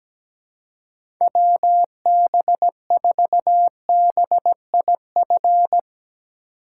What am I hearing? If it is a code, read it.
Morse code, heard as a keyed tone, about 17 wpm, 700 Hz. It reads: WB4BIF
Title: USA Call Signs - 6 Characters - 3x Character Spacing - 17wpm